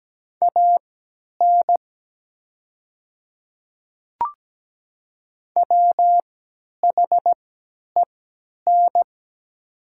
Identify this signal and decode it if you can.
Morse code, heard as a keyed tone, about 17 wpm, spaced as for 10 wpm, 700 Hz. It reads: AN WHEN
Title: 3x Character Spacing - Top 300 Words - 17wpm